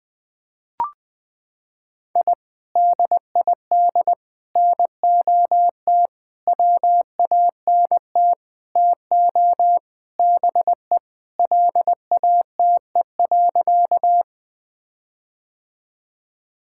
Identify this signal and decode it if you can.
Morse code, heard as a keyed tone, about 20 wpm, 700 Hz. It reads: I DID NOT WANT TO BE LATE.
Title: Sentences with 5-Word Phrases - 20wpm